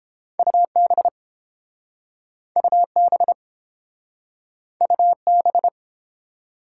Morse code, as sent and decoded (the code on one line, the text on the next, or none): ..- -... / ..- -... / ..- -...
UB UB UB